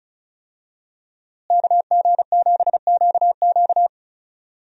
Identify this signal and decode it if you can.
Morse code, heard as a keyed tone, about 35 wpm, 700 Hz. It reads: KG7QQ
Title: USA Call Signs - 5 Characters - Repeated 2x - 35wpm